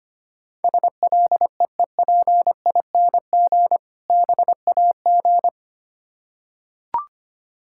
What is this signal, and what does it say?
Morse code, heard as a keyed tone, about 25 wpm, 700 Hz. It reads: SLEEPING BAG
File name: practice / 25wpm / Difficult Words with Hints - 25wpm